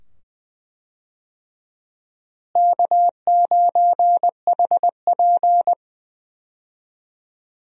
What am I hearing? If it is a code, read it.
Morse code, heard as a keyed tone, about 20 wpm, 700 Hz. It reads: K9HP
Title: USA Call Signs - 4 Characters - Repeated 2x - 20wpm